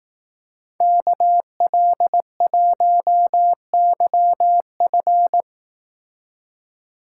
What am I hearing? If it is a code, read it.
Morse code, heard as a keyed tone, about 18 wpm, 700 Hz. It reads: KL1YF
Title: Speed Racing - USA Call Signs 15wpm